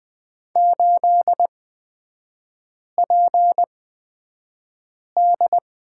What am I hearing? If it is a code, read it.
Morse code, heard as a keyed tone, about 20 wpm, spaced as for 5 wpm, 700 Hz. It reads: 8PD
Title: 243 Lesson 39e - Letter z - Three Characters 20wpm